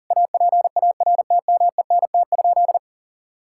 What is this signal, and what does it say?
Morse code, heard as a keyed tone, about 40 wpm, 700 Hz. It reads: APARTMENT?
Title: Sentences from Top 1000 Words - Review Part 6 - 40wpm